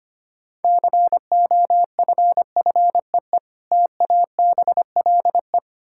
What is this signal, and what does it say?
Morse code, heard as a keyed tone, about 25 wpm, 700 Hz. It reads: COFFEE TABLE
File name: practice / 25wpm / Difficult Words - Speed-Racing with Hints - 25wpm